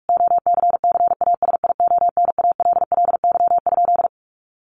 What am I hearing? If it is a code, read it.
Morse code, heard as a keyed tone, about 44 wpm, 700 Hz. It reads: OCCASIONALLY?
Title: Sentences from Top 1300 Words 45wpm